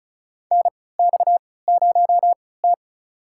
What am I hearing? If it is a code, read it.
Morse code, heard as a keyed tone, about 36 wpm, spaced as for 20 wpm, 700 Hz. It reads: NX0T
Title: USA Call Signs - 4 Characters - 3x Character Spacing - 35wpm